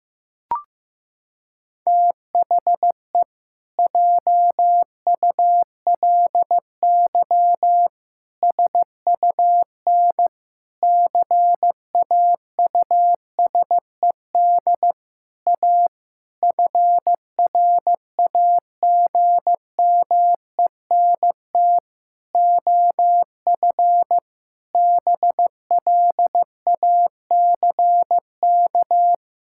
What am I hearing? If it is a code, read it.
Morse code, heard as a keyed tone, about 15 wpm, 700 Hz. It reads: THE JULY SUN CAUSED A FRAGMENT OF BLACK
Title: Pangram Challenge v2 - 15wpm